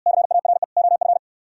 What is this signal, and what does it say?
Morse code, heard as a keyed tone, about 65 wpm, 700 Hz. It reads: CARE OF